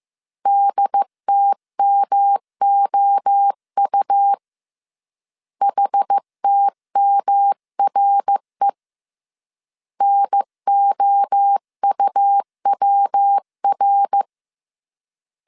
Morse code, heard as a keyed tone, about 15 wpm, 800 Hz. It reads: DTMOU HTMRE NOUWR
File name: CW / DET / 1 3 1500C S A 14wpm 2447